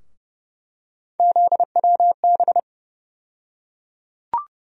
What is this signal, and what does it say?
Morse code, heard as a keyed tone, about 30 wpm, 700 Hz. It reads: ZWB
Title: USA Call Sign Postfixes - 3 Letters 30wpm